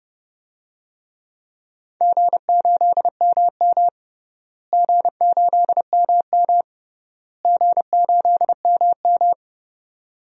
Morse code, heard as a keyed tone, about 30 wpm, 700 Hz. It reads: G8MM G8MM G8MM
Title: DXCC-100 Call Signs - 4 Characters - Repeated 3x - 30wpm